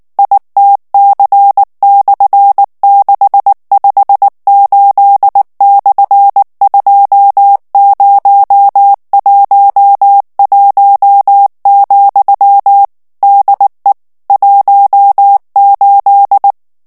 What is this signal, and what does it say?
Morse code, heard as a keyed tone, about 19 wpm, 800 Hz. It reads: ITC/658/2011, DE 18
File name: morse mp3 / ESPAGNOL TEXTE EN CLAIR 2 VITESSE 18 MOTS MINUTES